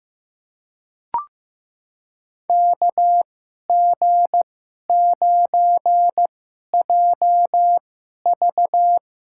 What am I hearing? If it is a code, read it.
Morse code, heard as a keyed tone, about 16 wpm, spaced as for 11 wpm, 700 Hz. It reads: KG9JV
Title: USA Call Signs - 2x Character Spacing - 15wpm